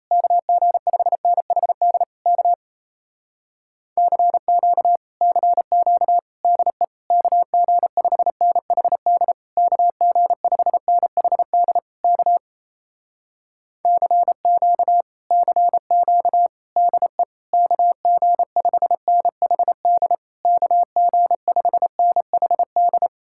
Morse code, heard as a keyed tone, about 33 wpm, 700 Hz. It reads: KG5NHD K CQ CQ DE KG5NHD KG5NHD K CQ CQ DE KG5NHD KG5NHD
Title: Speed-Racing Common QSO Phrases Encore - 28wpm